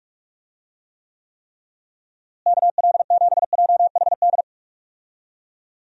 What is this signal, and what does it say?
Morse code, heard as a keyed tone, about 46 wpm, 700 Hz. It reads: KL7JHD
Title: USA Call Signs - 6 Characters - Repeated 3x - 45wpm